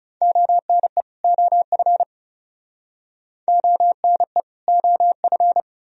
Morse code, heard as a keyed tone, about 40 wpm, 700 Hz. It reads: ONE OF ONE OF